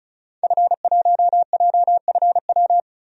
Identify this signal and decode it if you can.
Morse code, heard as a keyed tone, about 36 wpm, 700 Hz. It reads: F1JFW
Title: DXCC-100 Call Signs - 5 Characters - 35wpm